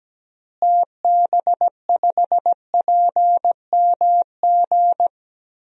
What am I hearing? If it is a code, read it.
Morse code, heard as a keyed tone, about 17 wpm, 700 Hz. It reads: TB5PMG
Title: DXCC-100 Call Signs - 6 Characters - Repeated 3x - 17wpm